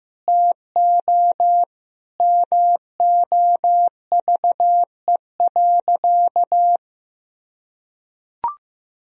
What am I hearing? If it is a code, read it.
Morse code, heard as a keyed tone, about 16 wpm, 700 Hz. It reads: TO MOVE.
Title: Sentences from Top 1000 Words - Review Part 2 - 15wpm